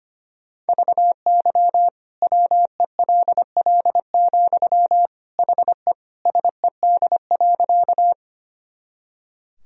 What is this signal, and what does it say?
Morse code, heard as a keyed tone, about 25 wpm, 700 Hz. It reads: VY WELL, HE SED.
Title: Sentences from Top 200 Words and Abbreviations 25wpm